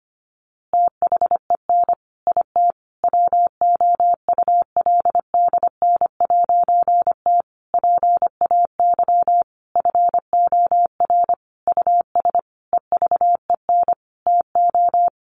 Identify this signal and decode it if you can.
Morse code, heard as a keyed tone, about 25 wpm, 700 Hz. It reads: THEN IT WOULDN'T PAY FOR US EVEN TO
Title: A Sound Of Thunder 25wpm